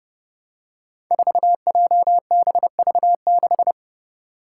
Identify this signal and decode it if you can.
Morse code, heard as a keyed tone, about 30 wpm, 700 Hz. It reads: 4JBV6
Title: Five Challenging Letter-Number Combinations 30wpm